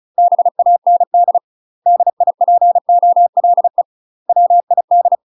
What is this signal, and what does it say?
Morse code, heard as a keyed tone, about 35 wpm, 700 Hz. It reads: BAND DIPOLE WID
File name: UK MASTER QSOs TRAINING / 35WPM / UK MASTER QSOs 0032